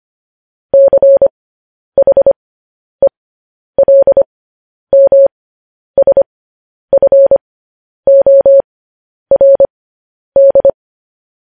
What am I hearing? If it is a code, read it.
Morse code, heard as a keyed tone, about 25 wpm, spaced as for 10 wpm, 550 Hz. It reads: CHELMSFORD
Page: news-headlines-in-morse-code-at-10-wpm